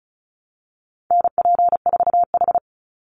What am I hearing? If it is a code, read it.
Morse code, heard as a keyed tone, about 34 wpm, 700 Hz. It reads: NP4H
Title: Call Signs Repeated 3x - 35wpm